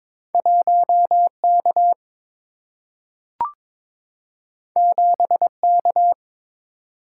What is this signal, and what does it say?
Morse code, heard as a keyed tone, about 22 wpm, 700 Hz. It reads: 1K 7K
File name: practice / 22wpm / Two Letter-Number Combinations - 22wpm